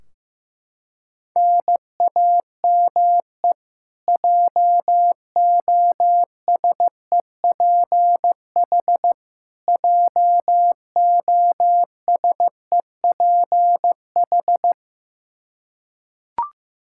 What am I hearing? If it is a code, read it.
Morse code, heard as a keyed tone, about 16 wpm, 700 Hz. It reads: NAME JOSEPH JOSEPH
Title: Practice QSO - Exchange 1 - Beginning - 15wpm